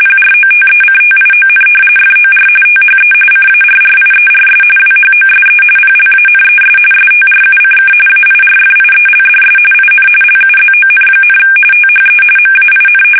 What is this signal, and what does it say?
RTTY, about 45.45 baud. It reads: PROTECTION FOR THE CARRIERS WHOSE PLANES STRUCK AIRFIELDS IN LUZON, THE MAIN
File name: fsk850 / battleshipnj